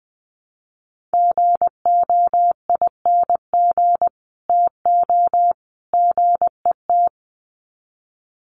Morse code, heard as a keyed tone, about 20 wpm, 700 Hz. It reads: GOING TO GET
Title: Sets of 3 Words 20wpm